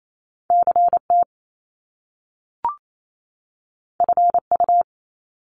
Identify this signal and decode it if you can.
Morse code, heard as a keyed tone, about 28 wpm, 700 Hz. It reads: CT FU
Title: Top Two Letter Combinations in Words 28wpm